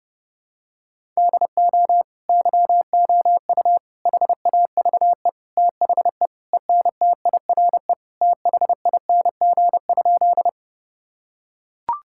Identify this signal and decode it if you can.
Morse code, heard as a keyed tone, about 29 wpm, 700 Hz. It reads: DO YOU HAVE THE ENTIRE THING?
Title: Sentences from Top 1000 Words - Review Part 5 - 30wpm